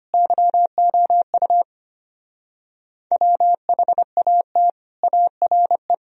Morse code, heard as a keyed tone, about 31 wpm, 700 Hz. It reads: YOU WHAT ARE
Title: Speed-Racing - Sets of 3 Words 25wpm